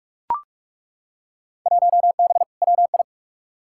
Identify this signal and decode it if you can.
Morse code, heard as a keyed tone, about 42 wpm, 700 Hz. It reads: 1B WI
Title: Field Day Exchanges Encore - 45wpm